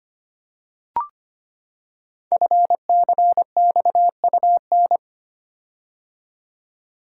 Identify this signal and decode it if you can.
Morse code, heard as a keyed tone, about 25 wpm, 700 Hz. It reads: FCXUN
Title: Code Groups - 5 Letters - 25wpm